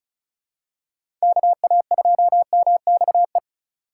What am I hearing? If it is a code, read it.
Morse code, heard as a keyed tone, about 34 wpm, 700 Hz. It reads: KA2MXE
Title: USA Call Signs - 6 Characters - 4x Character Spacing - 35wpm